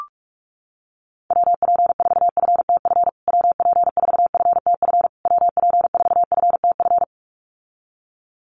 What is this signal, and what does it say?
Morse code, heard as a keyed tone, about 44 wpm, 700 Hz. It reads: WP4FTF WP4FTF WP4FTF
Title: USA Call Signs Repeated 3x 45wpm